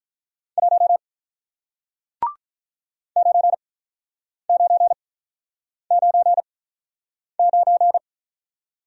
Morse code, heard as a keyed tone, about 48 wpm, spaced as for 9 wpm, 700 Hz. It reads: 1 9999